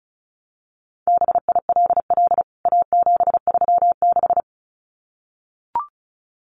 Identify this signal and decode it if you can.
Morse code, heard as a keyed tone, about 34 wpm, 700 Hz. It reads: BILL A736
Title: CWT Exchange 35wpm